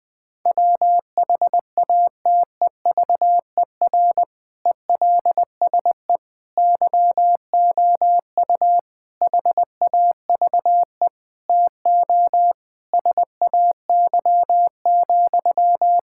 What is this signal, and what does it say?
Morse code, heard as a keyed tone, about 20 wpm, 700 Hz. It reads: WHATEVER ELSE YOU HAVE TO SAY,
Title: Sentences from Top 1600 Words 20wpm